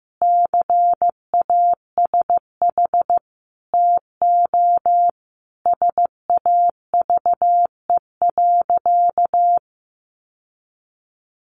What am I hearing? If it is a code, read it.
Morse code, heard as a keyed tone, about 15 wpm, 700 Hz. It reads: CASH TO SAVE.
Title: Sentences from Top 1100 Words 15wpm